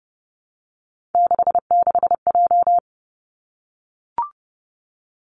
Morse code, heard as a keyed tone, about 30 wpm, 700 Hz. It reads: 66J
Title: Three Challenging Letter-Number Combinations 30wpm